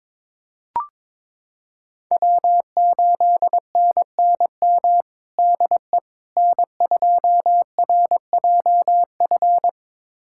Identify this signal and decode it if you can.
Morse code, heard as a keyed tone, about 22 wpm, 700 Hz. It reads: W8NNM DE N2RJF
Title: USA Callsign DE Callsign - 5 Characters - 22wpm